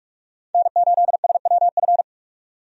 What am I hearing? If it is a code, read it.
Morse code, heard as a keyed tone, about 48 wpm, 700 Hz. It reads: N8SWF
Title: USA Call Signs - 5 Characters - 45wpm